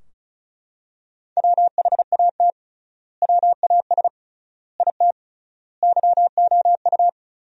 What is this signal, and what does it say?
Morse code, heard as a keyed tone, about 36 wpm, 700 Hz. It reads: WHAT WAS IT YOU
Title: 3x Word-Spacing - Sentences from Top 200 Words - 35wpm